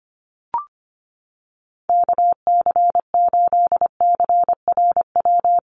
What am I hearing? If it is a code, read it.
Morse code, heard as a keyed tone, about 25 wpm, 700 Hz. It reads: KC8CRW
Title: USA Call Signs Repeated 2x 25wpm